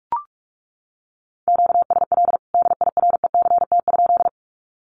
Morse code, heard as a keyed tone, about 44 wpm, 700 Hz. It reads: QSL DIRECT?
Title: Common QSO Phrases 45wpm